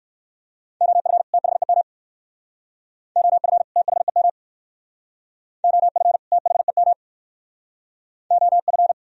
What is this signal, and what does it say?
Morse code, heard as a keyed tone, about 65 wpm, 690 Hz. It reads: OF THEM OF THEM OF THEM OF